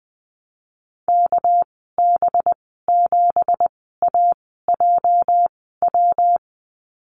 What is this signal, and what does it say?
Morse code, heard as a keyed tone, about 20 wpm, spaced as for 14 wpm, 700 Hz. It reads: KB7AJW
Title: 231 Lesson 37g - Letter x - Call Signs 20wpm